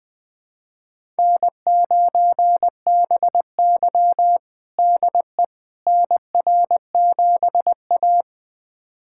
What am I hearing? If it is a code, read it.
Morse code, heard as a keyed tone, about 20 wpm, 700 Hz. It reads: N9BY DE NR7A
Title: USA Callsign DE Callsign - 4 Characters - 20wpm